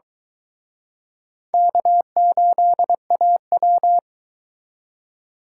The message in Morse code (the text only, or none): K8AW